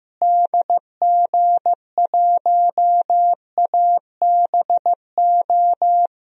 Morse code, heard as a keyed tone, about 15 wpm, 700 Hz. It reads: DG1ABO
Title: DXCC-100 Call Signs - 6 Characters - Repeated 3x - 15wpm